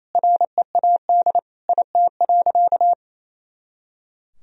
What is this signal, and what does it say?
Morse code, heard as a keyed tone, about 28 wpm, 700 Hz. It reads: READ IT.
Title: Sentences from Top 500 Words and Abbreviations 28wpm